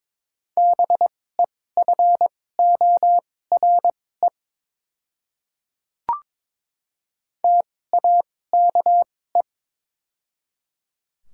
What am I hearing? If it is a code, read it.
Morse code, heard as a keyed tone, about 22 wpm, spaced as for 16 wpm, 700 Hz. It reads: BEFORE TAKE
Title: 2x Character Spacing - Top 200 Words - 22wpm